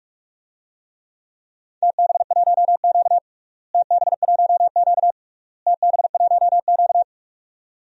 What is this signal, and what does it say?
Morse code, heard as a keyed tone, about 44 wpm, 700 Hz. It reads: TB1Q TB1Q TB1Q
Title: DXCC-100 Call Signs - 4 Characters - Repeated 3x - 45wpm